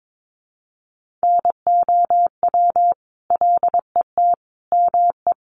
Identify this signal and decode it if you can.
Morse code, heard as a keyed tone, about 22 wpm, 700 Hz. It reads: NOW LET ME
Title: Sentences from Top 700 Words 22wpm